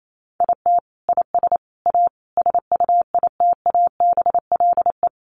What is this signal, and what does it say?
Morse code, heard as a keyed tone, about 28 wpm, 700 Hz. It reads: IT IS A SUITABLE
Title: Sentences from Top 1400 Words 28wpm